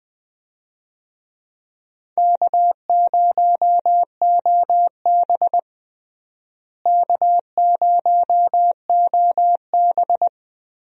Morse code, heard as a keyed tone, about 21 wpm, 700 Hz. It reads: K0OB K0OB
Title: USA Call Signs - 4 Characters - Repeated 3x - 20wpm